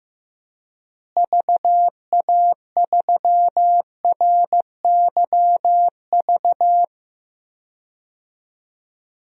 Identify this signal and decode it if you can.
Morse code, heard as a keyed tone, about 15 wpm, 700 Hz. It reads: VA3RYV
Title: DXCC-100 Call Signs - 6 Characters - Repeated 2x - 15wpm